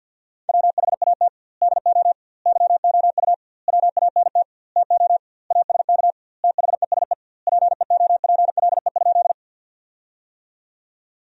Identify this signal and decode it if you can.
Morse code, heard as a keyed tone, about 50 wpm, 700 Hz. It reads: WHAT DO YOU WANT TO ASK THESE PEOPLE?
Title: 2x Word-Spacing - Sentences from Top 200 Words - 50wpm